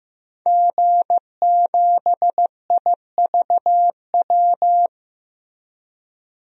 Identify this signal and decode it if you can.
Morse code, heard as a keyed tone, about 15 wpm, 700 Hz. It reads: G7IVW
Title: DXCC-100 Call Signs - 5 Characters - Repeated 3x - 15wpm